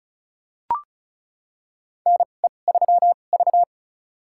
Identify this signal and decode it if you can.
Morse code, heard as a keyed tone, about 36 wpm, spaced as for 25 wpm, 700 Hz. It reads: NE3V
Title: USA Call Signs - 4 Characters - 2x Character Spacing - 35wpm